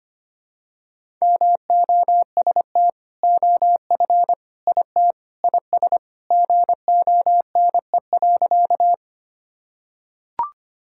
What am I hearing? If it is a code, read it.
Morse code, heard as a keyed tone, about 25 wpm, 700 Hz. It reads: MOST OF IT IS GONE.
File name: practice / 25wpm / Sentences with 2-Word Phrases - 25wpm